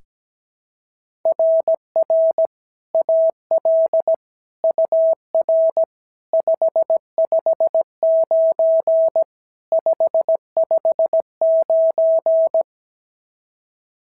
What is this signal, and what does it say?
Morse code, heard as a keyed tone, about 16 wpm, 650 Hz. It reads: RR AL UR 559 559
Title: SOTA QSO Practice 17wpm